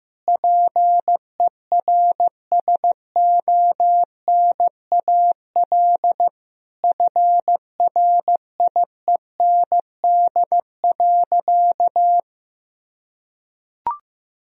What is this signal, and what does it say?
Morse code, heard as a keyed tone, about 16 wpm, 700 Hz. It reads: PERSONAL FRIEND.
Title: Sentences from Top 1000 Words - Review Part 4 - 15wpm